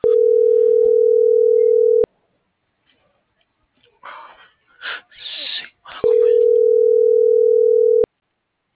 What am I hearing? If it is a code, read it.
Ringback tone (North America), 440 + 480 Hz.